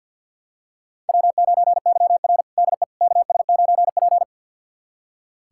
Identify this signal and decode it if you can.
Morse code, heard as a keyed tone, about 46 wpm, 700 Hz. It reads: W9YR DE KS9P
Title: USA Callsign DE Callsign - 4 Characters - 50wpm